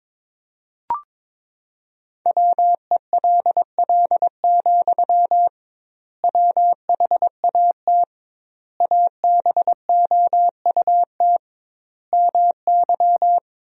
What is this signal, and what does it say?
Morse code, heard as a keyed tone, about 22 wpm, 700 Hz. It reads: WELL, WHAT ABOUT MY
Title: 2x Word-Spacing - Sentences from Top 500 Words - 22wpm